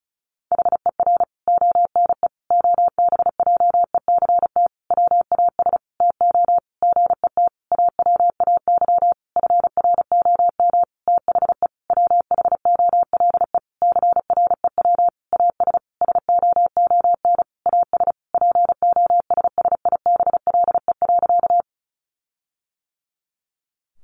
Morse code, heard as a keyed tone, about 34 wpm, 700 Hz. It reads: HER ONE OBJECT WAS TO GET AWAY FROM THE WHOLE CREW AS SOON AS POSSIBLE.
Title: Sentences from Top 1500 Words 35wpm